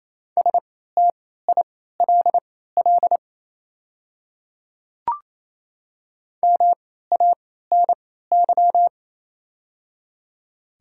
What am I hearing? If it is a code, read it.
Morse code, heard as a keyed tone, about 28 wpm, spaced as for 16 wpm, 700 Hz. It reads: STILL MANY
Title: 3x Character Spacing - Top 300 Words - 28wpm